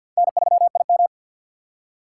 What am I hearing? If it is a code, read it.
Morse code, heard as a keyed tone, about 53 wpm, 700 Hz. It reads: N3IM
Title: USA Call Signs - 4 Characters - 2x Character Spacing - 50wpm